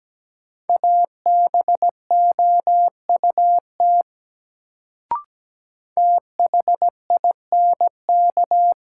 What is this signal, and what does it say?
Morse code, heard as a keyed tone, about 17 wpm, 700 Hz. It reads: ABOUT THINK
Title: Sets of 2 Words - New Word Drill - 17wpm